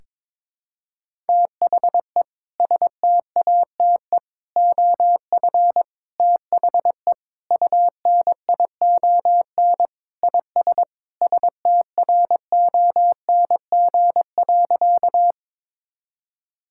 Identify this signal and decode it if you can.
Morse code, heard as a keyed tone, about 22 wpm, 700 Hz. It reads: THE STATE OF THE UNION IS STRONG.
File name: practice / 22wpm / Sentences with 3-Word Phrases - 22wpm